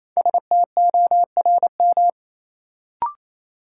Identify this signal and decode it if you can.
Morse code, heard as a keyed tone, about 28 wpm, 700 Hz. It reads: STORM
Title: Sentences from Top 1500 Words - New Word Drill - 28wpm